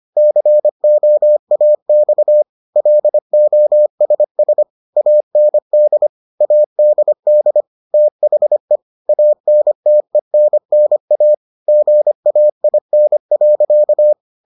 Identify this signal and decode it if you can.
Morse code, heard as a keyed tone, about 25 wpm, 600 Hz. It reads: COAX LOSS AND ADD THE ANTENNA GAIN.